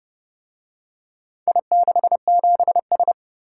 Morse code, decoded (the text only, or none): I67S